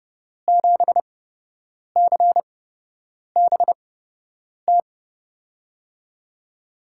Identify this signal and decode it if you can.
Morse code, heard as a keyed tone, about 30 wpm, spaced as for 8 wpm, 700 Hz. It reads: 7CBT
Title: USA Call Signs - 8x Character Spacing - 30wpm